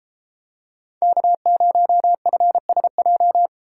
Morse code, heard as a keyed tone, about 33 wpm, 700 Hz. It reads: K0FSJ